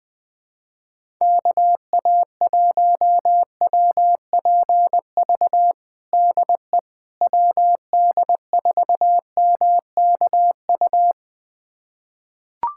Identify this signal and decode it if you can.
Morse code, heard as a keyed tone, about 21 wpm, 700 Hz. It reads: KA1WPV DE WD4MKU
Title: USA Callsign DE Callsign - 6 Characters - 20wpm